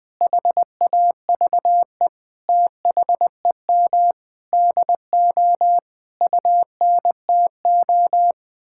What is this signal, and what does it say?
Morse code, heard as a keyed tone, about 20 wpm, 700 Hz. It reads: HAVE THEM DO UNTO
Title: Sentences with 8-Word Phrases - 20wpm